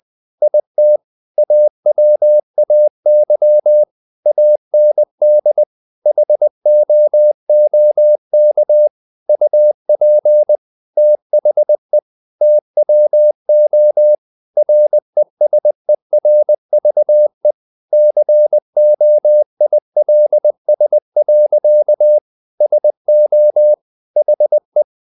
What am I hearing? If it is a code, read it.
Morse code, heard as a keyed tone, about 20 wpm, 600 Hz. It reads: IT AWAY AND HOOK UP THE TWO RESERVE COILS. SO HE
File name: the old man and the sea / the old man and the sea part 1 20wpm